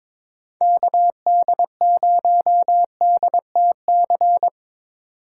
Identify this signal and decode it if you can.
Morse code, heard as a keyed tone, about 22 wpm, 700 Hz. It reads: KD0DTC
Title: USA Call Signs - 6 Characters - 4x Character Spacing - 22wpm